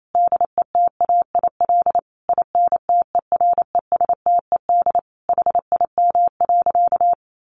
Morse code, heard as a keyed tone, about 28 wpm, 700 Hz. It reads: DETAIL INTERESTED HIM.
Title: Sentences from Top 1500 Words 28wpm